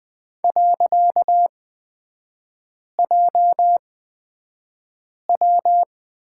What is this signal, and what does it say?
Morse code, heard as a keyed tone, about 20 wpm, spaced as for 5 wpm, 700 Hz. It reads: .JW
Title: 255 Lesson 41e - Period - Three Characters 20wpm